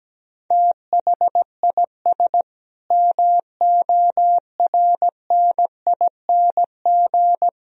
Morse code, heard as a keyed tone, about 17 wpm, 700 Hz. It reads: THIS MORNING
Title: Sentences from Top 1000 Words - Review Part 3 - 17wpm